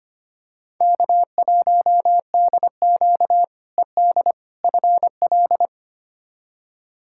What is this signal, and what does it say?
Morse code, heard as a keyed tone, about 25 wpm, 700 Hz. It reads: K1DQ ED FL
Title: K1USN SST QSO Practice - 25wpm